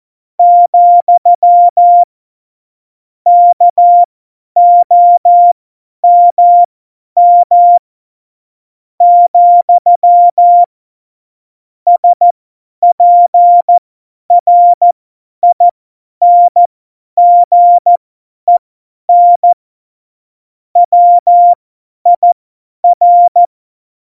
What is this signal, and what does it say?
Morse code, heard as a keyed tone, about 14 wpm, spaced as for 9 wpm, 700 Hz. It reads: , KOMM, SPRINGEN WIR
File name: Morsekurs / Witze / Witz 14. August 2018, 50 F70